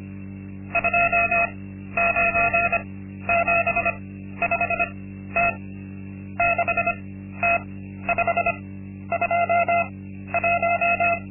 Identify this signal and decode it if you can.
Morse code, heard as a keyed tone, about 25 wpm, spaced as for 12 wpm, 690 Hz. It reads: 2975T 6T521
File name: Linksmakalnis-2